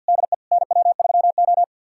Morse code, heard as a keyed tone, about 51 wpm, 700 Hz. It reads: DE NW3O
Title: USA Callsign DE Callsign - 4 Characters - 50wpm